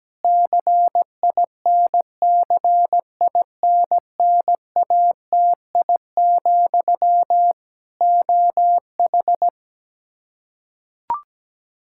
Morse code, heard as a keyed tone, about 17 wpm, 700 Hz. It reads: CINCINNATI, OH
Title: K1USN SST QSO Practice 17wpm